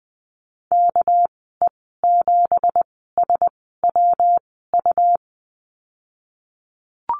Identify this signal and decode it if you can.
Morse code, heard as a keyed tone, about 20 wpm, spaced as for 15 wpm, 700 Hz. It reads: KE7SWU
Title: 200 Lesson 32g - Letter k - Call Signs 20wpm